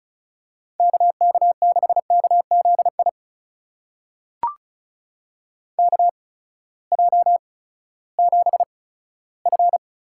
Morse code, heard as a keyed tone, about 36 wpm, spaced as for 18 wpm, 700 Hz. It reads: KK6KZI KJ7F
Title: USA Call Signs - 6 Characters - 8x Character Spacing - 35wpm